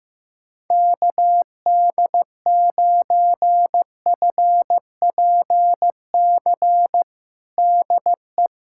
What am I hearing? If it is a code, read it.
Morse code, heard as a keyed tone, about 15 wpm, 700 Hz. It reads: KD9FPC DE
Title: USA Callsign DE Callsign - 6 Characters - 15wpm